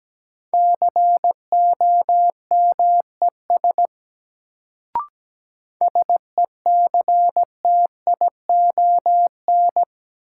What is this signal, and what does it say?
Morse code, heard as a keyed tone, about 17 wpm, 700 Hz. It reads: COMES SECTION
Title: Sentences from Top 1000 Words - Review Part 5 - New Word Drill - 17wpm